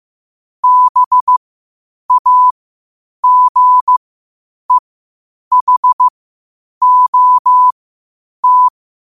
Morse code, heard as a keyed tone, about 15 wpm, spaced as for 8 wpm, 1000 Hz. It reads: BAGEHOT
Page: qotd-at-8-wpm